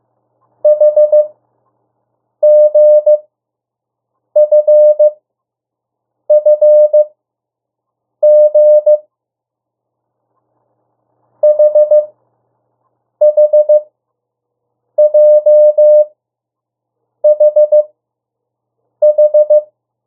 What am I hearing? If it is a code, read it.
Morse code, heard as a keyed tone, about 15 wpm, spaced as for 6 wpm, 600 Hz. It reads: HGFFG HHJHH
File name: morse mp3 / 02 urok 1